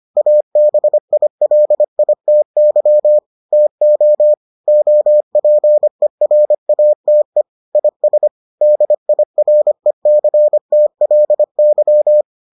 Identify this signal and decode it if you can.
Morse code, heard as a keyed tone, about 25 wpm, 600 Hz. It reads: ABILITY TO OPERATE IS DIRECTLY